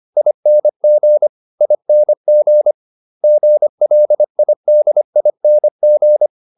Morse code, heard as a keyed tone, about 25 wpm, 600 Hz. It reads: ING ING GLIDING